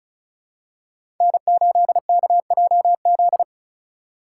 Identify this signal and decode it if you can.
Morse code, heard as a keyed tone, about 36 wpm, 700 Hz. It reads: N8KJZ
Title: Code Groups - 5 Letter:Numbers - 35wpm